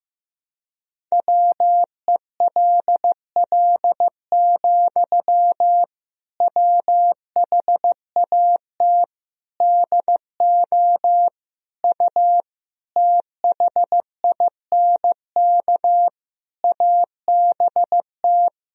Morse code, heard as a keyed tone, about 15 wpm, 700 Hz. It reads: WELL, WHAT DO U THINK ABT
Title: Sentences from Top 200 Words and Abbreviations 15wpm